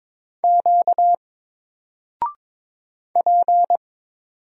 Morse code, heard as a keyed tone, about 22 wpm, 700 Hz. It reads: Q P
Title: Single Character (Letter:Number:Punctuation:Prosign) 22wpm